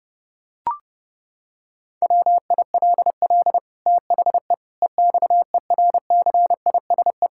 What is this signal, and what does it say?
Morse code, heard as a keyed tone, about 30 wpm, 700 Hz. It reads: WILL THE EXERCISE
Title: Sentences from Top 1000 Words - Review Part 6 - 30wpm